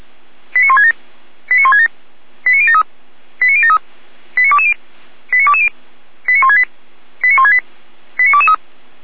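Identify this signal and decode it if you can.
Five-tone selective call (ZVEI), heard as quick runs of five tones, tones of about 70 ms. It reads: 89167 89167 89073 89073 89220 89220 89168 89168 89223
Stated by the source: ZVEI five-tone call received by SDR.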